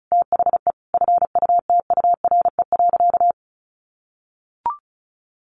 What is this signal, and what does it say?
Morse code, heard as a keyed tone, about 34 wpm, 700 Hz. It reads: THE FUTURE.
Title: Sentences from Top 1400 Words 35wpm